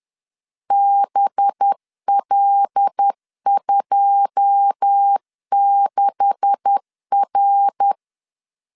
Morse code, heard as a keyed tone, about 10 wpm, 800 Hz. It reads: BL26R